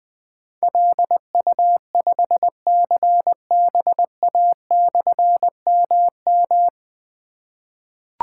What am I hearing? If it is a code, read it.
Morse code, heard as a keyed tone, about 20 wpm, 700 Hz. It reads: LU5CBA/MM
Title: Difficult Call Signs 20wpm